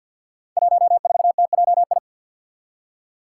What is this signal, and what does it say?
Morse code, heard as a keyed tone, about 51 wpm, 700 Hz. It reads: 14TJI
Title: Five Challenging Letter-Number Combinations 50wpm